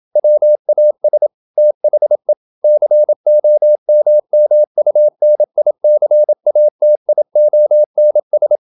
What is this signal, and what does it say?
Morse code, heard as a keyed tone, about 27 wpm, 600 Hz. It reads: WAS THE COMMUNICATIONS
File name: Never 0027wpm